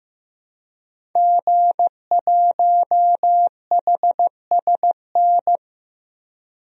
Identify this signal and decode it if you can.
Morse code, heard as a keyed tone, about 15 wpm, 700 Hz. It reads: G1HSN
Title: DXCC-100 Call Signs - 5 Characters - 15wpm